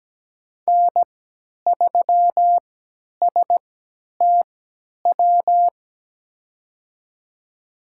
Morse code, heard as a keyed tone, about 17 wpm, spaced as for 10 wpm, 700 Hz. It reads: N3STW
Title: USA Call Signs - 5 Characters - 3x Character Spacing - 17wpm